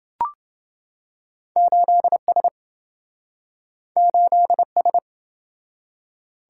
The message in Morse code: 8S 8S

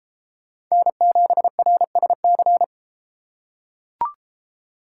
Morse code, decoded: N7RSC

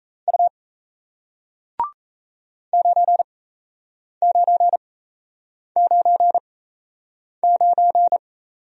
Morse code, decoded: U 9999

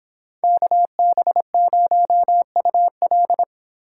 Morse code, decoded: KB0UL

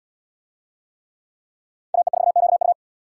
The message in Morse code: AE4QU

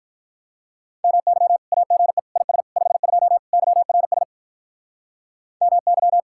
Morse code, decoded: MY AGE IS 52 YRS MY

